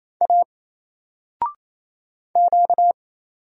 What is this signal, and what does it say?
Morse code, heard as a keyed tone, about 27 wpm, 700 Hz. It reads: A Q
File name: Single-Letter-Number-28wpm